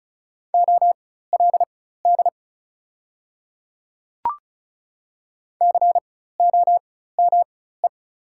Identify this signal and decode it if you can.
Morse code, heard as a keyed tone, about 34 wpm, spaced as for 16 wpm, 700 Hz. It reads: OLD COME
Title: 4x Character Spacing - Top 200 Words - 35wpm